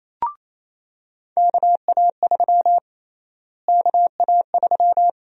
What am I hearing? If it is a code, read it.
Morse code, heard as a keyed tone, about 28 wpm, 700 Hz. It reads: KA3 KA3
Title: USA Call Sign Prefixes - Repeated 2x - 28wpm